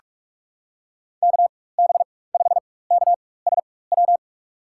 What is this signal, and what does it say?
Morse code, heard as a keyed tone, about 43 wpm, spaced as for 21 wpm, 700 Hz. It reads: KB5KSW
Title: USA Call Signs - 4x Character Spacing - 45wpm